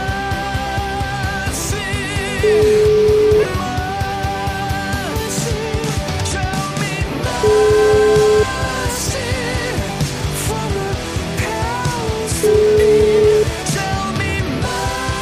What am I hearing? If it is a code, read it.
Ringback tone (Europe, 425 Hz), with noise.